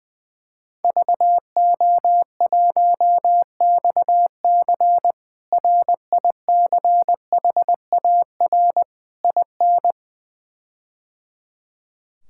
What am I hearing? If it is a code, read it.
Morse code, heard as a keyed tone, about 21 wpm, 700 Hz. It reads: VO1XC RICHAR IN
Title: K1USN SST QSO Practice - 20wpm